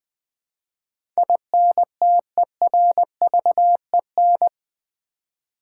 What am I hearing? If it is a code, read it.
Morse code, heard as a keyed tone, about 20 wpm, 700 Hz. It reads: INTERVEN
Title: Build A Word - 4 Letters At A Time - 12-Letter Words - 20wpm